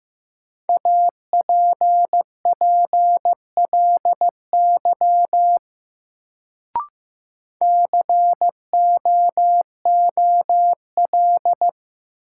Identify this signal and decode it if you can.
Morse code, heard as a keyed tone, about 15 wpm, 700 Hz. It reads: APPLY COOL
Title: Sentences from Top 1200 Words - New Word Drill - 15wpm